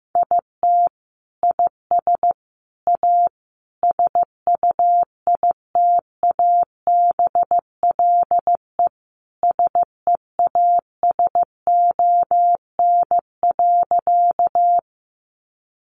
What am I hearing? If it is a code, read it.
Morse code, heard as a keyed tone, about 15 wpm, 700 Hz. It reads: IT IS A SUITABLE SEASON.
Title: Sentences from Top 1400 Words 15wpm